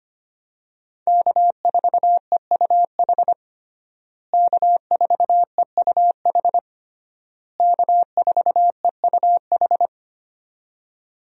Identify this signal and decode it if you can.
Morse code, heard as a keyed tone, about 25 wpm, 700 Hz. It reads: K4EUH K4EUH K4EUH
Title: USA Call Signs - 5 Characters - Repeated 3x - 25wpm